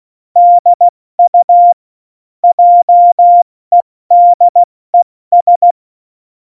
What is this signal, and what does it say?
Morse code, heard as a keyed tone, about 16 wpm, 700 Hz. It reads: DU JEDES
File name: Morsekurs / Witze / Witz 11. Juni 2019, 70 F80